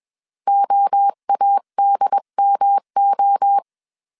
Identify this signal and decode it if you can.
Morse code, heard as a keyed tone, about 21 wpm, 800 Hz. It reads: OADMO